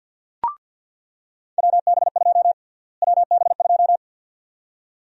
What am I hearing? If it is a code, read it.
Morse code, heard as a keyed tone, about 47 wpm, 700 Hz. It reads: WB2 WB2
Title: USA Call Sign Prefixes - Repeated 2x - 50wpm